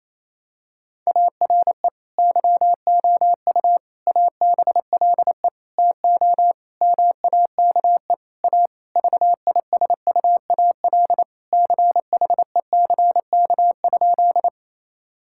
Morse code, heard as a keyed tone, about 28 wpm, 700 Hz. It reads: ARE YOU ABLE TO MAKE A VISUAL CHECK?
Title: Sentences from Top 1000 Words - Review Part 6 - 28wpm